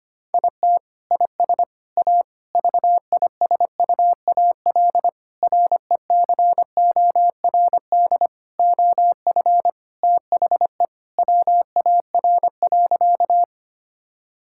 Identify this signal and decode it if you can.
Morse code, heard as a keyed tone, about 25 wpm, 700 Hz. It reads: IT IS A VISUAL RECORD OF THE WAR.
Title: Sentences from Top 1000 Words - Review Part 6 - 25wpm